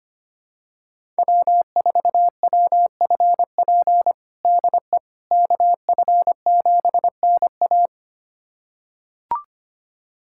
Morse code, heard as a keyed tone, about 25 wpm, 700 Hz. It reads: W4WFP DE KF7NA
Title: USA Callsign DE Callsign - 5 Characters - 25wpm